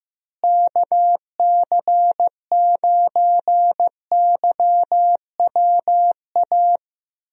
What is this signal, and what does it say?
Morse code, heard as a keyed tone, about 15 wpm, 700 Hz. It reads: KC9YWA
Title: USA Call Signs - 6 Characters - Repeated 2x - 15wpm